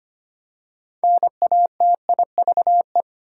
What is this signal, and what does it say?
Morse code, heard as a keyed tone, about 25 wpm, 700 Hz. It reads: NATIVE
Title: Six Letter Words - 25wpm